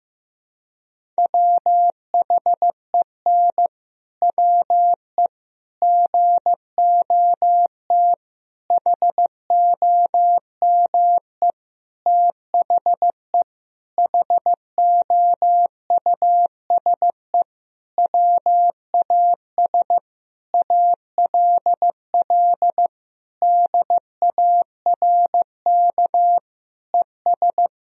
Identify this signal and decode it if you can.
Morse code, heard as a keyed tone, about 15 wpm, 700 Hz. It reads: WHEN WE GOT HOME THE HOUSE WAS ALL DARK ES
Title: Sentences from Top 500 Words and Abbreviations 15wpm